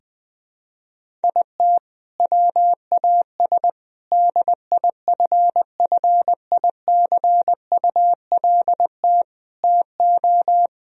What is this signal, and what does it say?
Morse code, heard as a keyed tone, about 20 wpm, 700 Hz. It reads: IT WAS DIFFICULT TO
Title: Sentences from Top 1000 Words - Review Part 1 - 20wpm